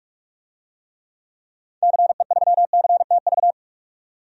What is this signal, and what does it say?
Morse code, heard as a keyed tone, about 46 wpm, 700 Hz. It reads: CE3CTV
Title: DXCC-100 Call Signs - 6 Characters - 45wpm